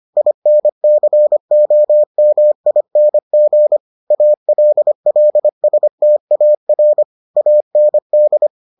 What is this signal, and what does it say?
Morse code, heard as a keyed tone, about 25 wpm, 600 Hz. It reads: INCOMING ALLSTAR AND